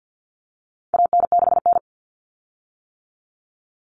Morse code, heard as a keyed tone, about 50 wpm, 700 Hz. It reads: AN6N
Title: Common Cut Numbers - 4-Digit 50wpm